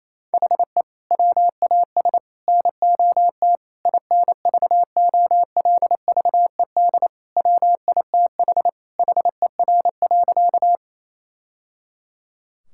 Morse code, heard as a keyed tone, about 28 wpm, 700 Hz. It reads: HE WAS NOT INVOLVED WITH HER.
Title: Sentences from Top 1000 Words - Review Part 5 - 28wpm